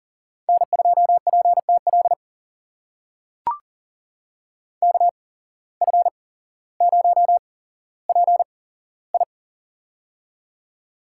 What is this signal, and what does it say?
Morse code, heard as a keyed tone, about 40 wpm, spaced as for 17 wpm, 700 Hz. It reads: N2PTL KF0PI
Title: USA Call Signs - 5 Characters - 8x Character Spacing - 40wpm